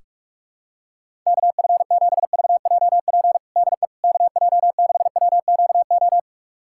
Morse code, heard as a keyed tone, about 48 wpm, 700 Hz. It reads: KF7VJP DE KJ6WQO
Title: USA Callsign DE Callsign 45wpm